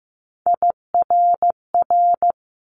Morse code, heard as a keyed tone, about 15 wpm, 700 Hz. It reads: IRR
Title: Multi-letter Phonemes 15wpm